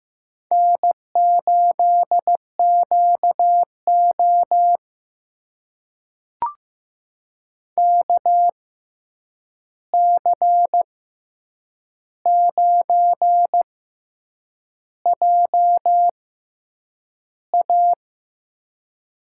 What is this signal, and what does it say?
Morse code, heard as a keyed tone, about 15 wpm, spaced as for 7 wpm, 700 Hz. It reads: N8QO KC9JA
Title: USA Call Signs - 6x Character Spacing - 15wpm